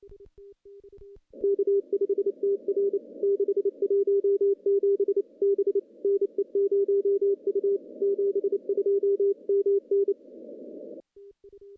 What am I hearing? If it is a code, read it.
Morse code, heard as a keyed tone, about 29 wpm, 400 Hz. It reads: K5TR 617 B NE0U 72 MN
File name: contests / CW-SS-k5tr- -1400